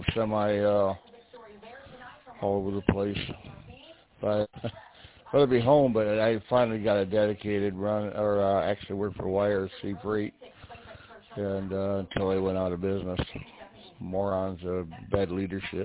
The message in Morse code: none

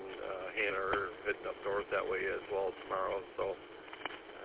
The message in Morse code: none

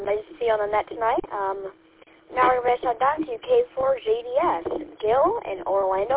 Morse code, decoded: none

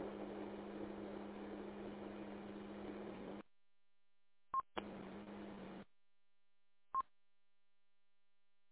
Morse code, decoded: none

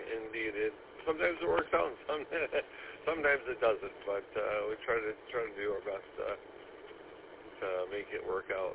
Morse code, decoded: none

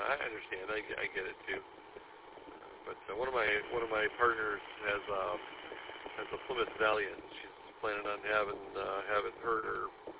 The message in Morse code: none

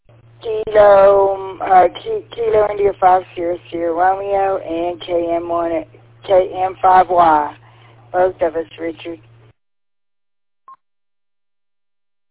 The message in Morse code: none

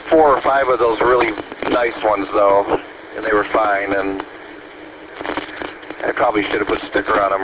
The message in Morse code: none

